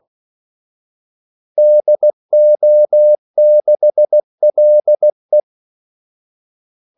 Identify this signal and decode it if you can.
Morse code, heard as a keyed tone, about 16 wpm, 600 Hz. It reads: DO6LE
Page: Kaldesignaler 16 wpm